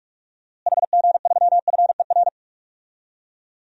Morse code, heard as a keyed tone, about 45 wpm, 700 Hz. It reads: HG3FER
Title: DXCC-100 Call Signs - 6 Characters - Repeated 2x - 45wpm